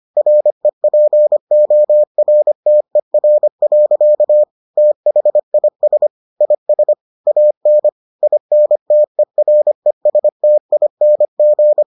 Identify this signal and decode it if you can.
Morse code, heard as a keyed tone, about 25 wpm, 600 Hz. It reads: REPORTER. THIS IS AN INTERESTING